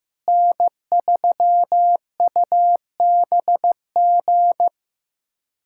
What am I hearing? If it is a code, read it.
Morse code, heard as a keyed tone, about 15 wpm, 700 Hz. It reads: N3UBG
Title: USA Callsign DE Callsign - 5 Characters - 15wpm